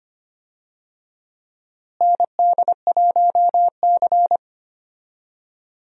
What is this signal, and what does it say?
Morse code, heard as a keyed tone, about 25 wpm, 700 Hz. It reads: ND1C
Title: USA Call Signs - 4 Characters - Repeated 3x - 25wpm